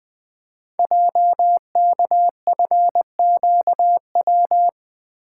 Morse code, heard as a keyed tone, about 20 wpm, 700 Hz. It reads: JKFQW
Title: Code Groups - 5 Letters - 20wpm